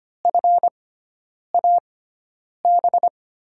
Morse code, heard as a keyed tone, about 25 wpm, spaced as for 9 wpm, 690 Hz. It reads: FAB
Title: USA Call Signs - 5 Characters - 6x Character Spacing - 25wpm